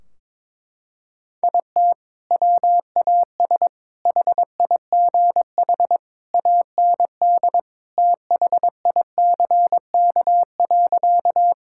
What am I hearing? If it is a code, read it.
Morse code, heard as a keyed tone, about 22 wpm, 700 Hz. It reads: IT WAS HIGH AND THICK.
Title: Sentences from Top 1600 Words 22wpm